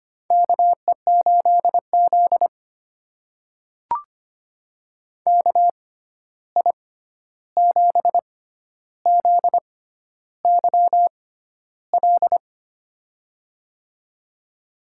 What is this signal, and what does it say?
Morse code, heard as a keyed tone, about 25 wpm, spaced as for 11 wpm, 700 Hz. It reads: KE8Z KI7ZYL
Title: USA Call Signs - 6x Character Spacing - 25wpm